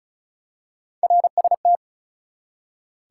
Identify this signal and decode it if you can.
Morse code, heard as a keyed tone, about 33 wpm, 700 Hz. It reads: RST
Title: Top 180 QSO Elements 35wpm